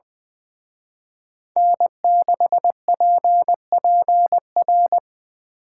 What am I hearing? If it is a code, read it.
Morse code, heard as a keyed tone, about 24 wpm, 700 Hz. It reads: N6PPR
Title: Speed Racing - USA 1x3 Call Signs 20wpm